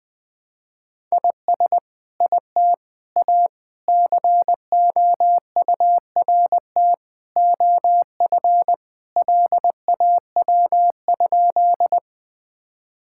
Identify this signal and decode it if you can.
Morse code, heard as a keyed tone, about 21 wpm, 700 Hz. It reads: IS IT A COURT OF LAW?
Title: Sentences from Top 1600 Words 20wpm